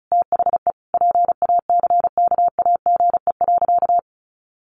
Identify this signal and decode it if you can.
Morse code, heard as a keyed tone, about 34 wpm, 700 Hz. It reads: THE PACKAGE.
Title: Additional Easy Phrases 35wpm